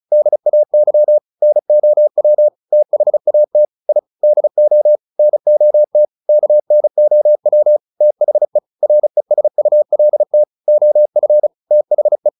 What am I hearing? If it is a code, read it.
Morse code, heard as a keyed tone, about 34 wpm, 600 Hz. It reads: DAY NOW THAT I DO NOT KNOW THE RESULT OF THE
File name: the old man and the sea / the old man and the sea 1 35wpm